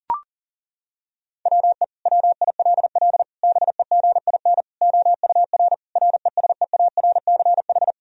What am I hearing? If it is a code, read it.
Morse code, heard as a keyed tone, about 40 wpm, 700 Hz. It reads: WE WILL BEGIN OUR RESEARCH
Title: Sentences from Top 1000 Words - Review Part 4 - 40wpm